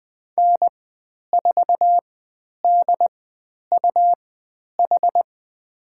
Morse code, heard as a keyed tone, about 20 wpm, spaced as for 10 wpm, 700 Hz. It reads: N4DUH
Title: 102 Lesson 17g - Letter u - Call Signs 20wpm